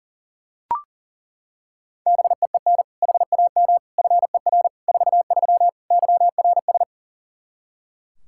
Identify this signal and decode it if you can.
Morse code, heard as a keyed tone, about 40 wpm, 700 Hz. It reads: BEEN HAM FER 43 YRS
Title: Practice QSO - Exchange 3 - More Info - 40wpm